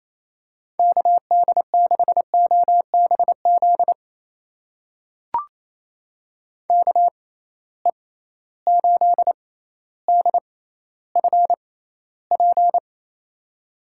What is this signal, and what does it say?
Morse code, heard as a keyed tone, about 28 wpm, spaced as for 14 wpm, 700 Hz. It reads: KD6OBZ KE8DFP
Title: USA Call Signs - 6 Characters - 6x Character Spacing - 28wpm